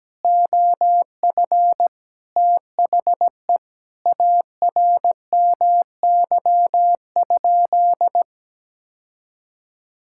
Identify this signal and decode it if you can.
Morse code, heard as a keyed tone, about 17 wpm, 700 Hz. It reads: OF THE ARMY?
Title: Sentences from Top 1000 Words - Review Part 6 - 17wpm